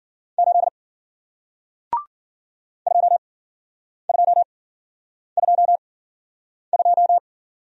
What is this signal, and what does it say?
Morse code, heard as a keyed tone, about 54 wpm, spaced as for 9 wpm, 700 Hz. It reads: 8 2222